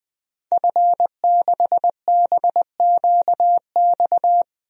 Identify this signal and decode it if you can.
Morse code, heard as a keyed tone, about 20 wpm, 700 Hz. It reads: F6BQX
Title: DXCC-100 Call Signs - 5 Characters - 20wpm